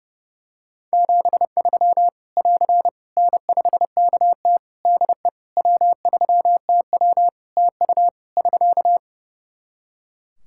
Morse code, heard as a keyed tone, about 32 wpm, 700 Hz. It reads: 73 <AR> N5KT DE W3TW TU <SK>
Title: Practice QSO - Exchange 4 - Ending - 30wpm